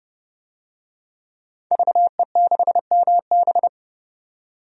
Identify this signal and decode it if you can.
Morse code, heard as a keyed tone, about 30 wpm, 700 Hz. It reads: VE6MB
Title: DXCC-100 Call Signs - 5 Characters - 30wpm